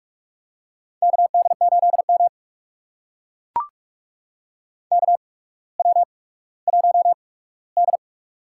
Morse code, heard as a keyed tone, about 43 wpm, spaced as for 19 wpm, 700 Hz. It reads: KD8M KW1D
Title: USA Call Signs - 4 Characters - 8x Character Spacing - 45wpm